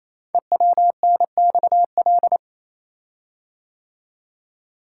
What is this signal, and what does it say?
Morse code, heard as a keyed tone, about 27 wpm, 700 Hz. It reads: EWNXL
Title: Code Groups - 5 Letters - 28wpm